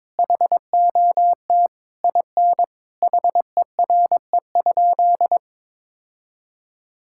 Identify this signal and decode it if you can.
Morse code, heard as a keyed tone, about 22 wpm, 700 Hz. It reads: HOT IN HERE?
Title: Sentences from Top 1000 Words - Review Part 6 - 22wpm